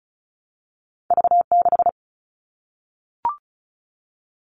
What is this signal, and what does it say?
Morse code, heard as a keyed tone, about 36 wpm, 700 Hz. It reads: V6
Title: All Cut Numbers - 2-Digit 35wpm